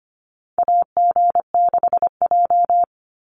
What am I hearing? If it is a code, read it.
Morse code, heard as a keyed tone, about 25 wpm, 690 Hz. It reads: AG6J
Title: USA Call Signs 25wpm